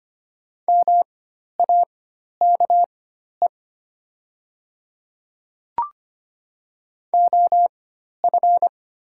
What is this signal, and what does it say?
Morse code, heard as a keyed tone, about 25 wpm, spaced as for 12 wpm, 700 Hz. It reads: MAKE OF
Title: 4x Character Spacing - Top 100 Words - 25wpm